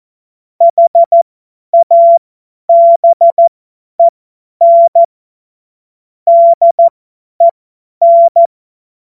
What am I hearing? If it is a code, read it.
Morse code, heard as a keyed tone, about 14 wpm, spaced as for 10 wpm, 680 Hz. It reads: HABEN DEN
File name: Morsekurs / Witze / Witz 21. Juni 2022 50 F70